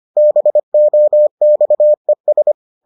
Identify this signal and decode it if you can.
Morse code, heard as a keyed tone, about 25 wpm, 600 Hz. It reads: BOXES